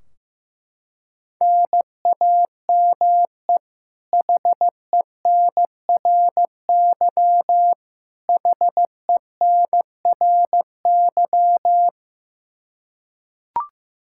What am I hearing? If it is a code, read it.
Morse code, heard as a keyed tone, about 15 wpm, 700 Hz. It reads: NAME HENRY HENRY
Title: Practice QSO - Exchange 1 - Beginning - 15wpm